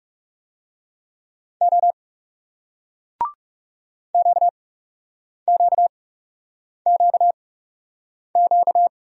Morse code, heard as a keyed tone, about 40 wpm, spaced as for 8 wpm, 700 Hz. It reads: O QQQQ